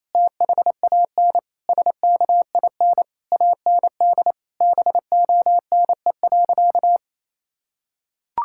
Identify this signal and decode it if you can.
Morse code, heard as a keyed tone, about 28 wpm, 700 Hz. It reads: THAN SKIN AND BONE.
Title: Sentences from Top 1600 Words 28wpm